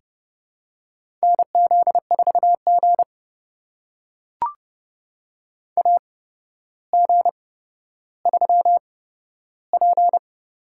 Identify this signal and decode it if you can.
Morse code, heard as a keyed tone, about 30 wpm, spaced as for 13 wpm, 700 Hz. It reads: NZ4G AG3P
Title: USA Call Signs - 4 Characters - 8x Character Spacing - 30wpm